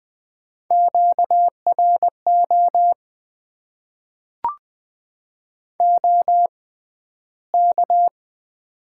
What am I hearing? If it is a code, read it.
Morse code, heard as a keyed tone, about 20 wpm, spaced as for 12 wpm, 700 Hz. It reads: QRO OK
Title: Top 90 QSO Elements - 6x Character Spacing - 20wpm